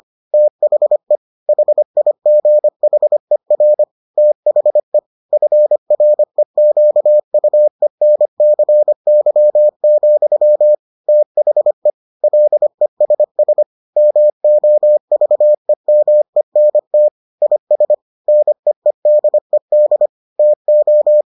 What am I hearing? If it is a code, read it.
Morse code, heard as a keyed tone, about 25 wpm, 600 Hz. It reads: THE HIGHER THE FREQUENCY, THE LESS MOVEMENT IS NEEDED TO